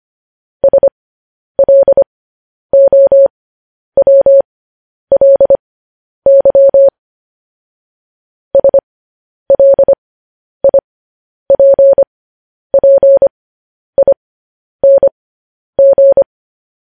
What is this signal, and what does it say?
Morse code, heard as a keyed tone, about 25 wpm, spaced as for 10 wpm, 550 Hz. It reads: SLOWLY SLIPPING
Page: news headlines in morse code at 10 wpm